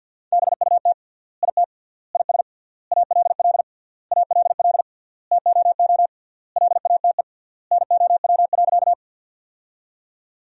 Morse code, heard as a keyed tone, about 47 wpm, 700 Hz. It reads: BUT IT IS ALL ALL TOO LATE NOW.
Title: 3x Word-Spacing - Sentences from Top 300 Words - 50wpm